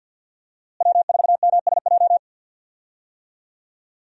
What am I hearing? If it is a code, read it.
Morse code, heard as a keyed tone, about 50 wpm, 700 Hz. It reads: W4MSJ
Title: USA Call Signs - 5 Characters - Repeated 2x - 50wpm